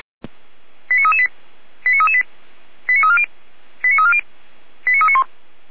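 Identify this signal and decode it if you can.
Five-tone selective call (ZVEI), heard as quick runs of five tones, tones of about 70 ms. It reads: 89338 89338 89340 89340 89391
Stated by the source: ZVEI five-tone selective call received by SDR, logged by Feuerwehr Orlamünde fire brigade.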